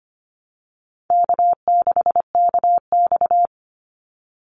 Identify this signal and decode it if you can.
Morse code, heard as a keyed tone, about 25 wpm, 700 Hz. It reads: K6KX
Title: Call Signs Repeated 2x - 25wpm